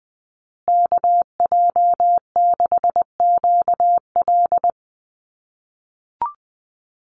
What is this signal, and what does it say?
Morse code, heard as a keyed tone, about 20 wpm, 700 Hz. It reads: KJ6QL
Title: USA Call Signs Repeated 3x 20wpm